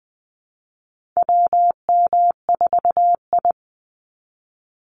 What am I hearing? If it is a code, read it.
Morse code, heard as a keyed tone, about 20 wpm, 700 Hz. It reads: WM4I
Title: Call Signs Repeated 3x - 20wpm